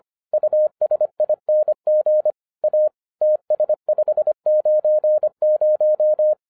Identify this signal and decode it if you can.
Morse code, heard as a keyed tone, about 25 wpm, 610 Hz. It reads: USING A TS590